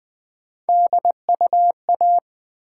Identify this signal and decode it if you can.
Morse code, heard as a keyed tone, about 20 wpm, 700 Hz. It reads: DUA
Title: Build A Word - 1 Letter At A Time - 4-Letter Words - 20wpm